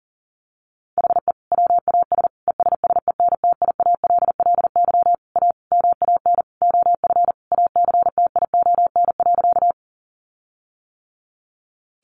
Morse code, heard as a keyed tone, about 40 wpm, 700 Hz. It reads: HE WAS ESSENTIALLY A MAN OF ACTION.
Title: Sentences from Top 1300 Words 40wpm